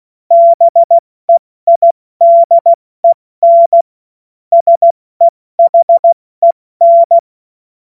Morse code, heard as a keyed tone, about 16 wpm, spaced as for 14 wpm, 680 Hz. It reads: BEIDEN SEHEN
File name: Morsekurs / Witze / Witz 07. Dezember 2021, 70 F80